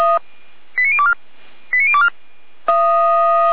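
Five-tone selective call (ZVEI), heard as quick runs of five tones, tones of about 70 ms. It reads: 89024 89024
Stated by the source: Five-tone ZVEI selective call 89337, received by SDR, from the feuerwehr-orlamuende.org fire brigade audio archive.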